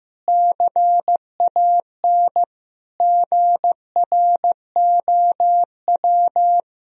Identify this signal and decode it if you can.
Morse code, heard as a keyed tone, about 15 wpm, 700 Hz. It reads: CAN GROW
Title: Sentences from Top 1000 Words - Review Part 1 - 15wpm